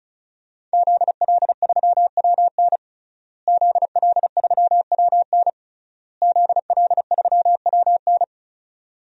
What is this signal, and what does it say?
Morse code, heard as a keyed tone, about 36 wpm, 700 Hz. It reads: ZL3WN ZL3WN ZL3WN
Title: DXCC-100 Call Signs - 5 Characters - Repeated 3x - 35wpm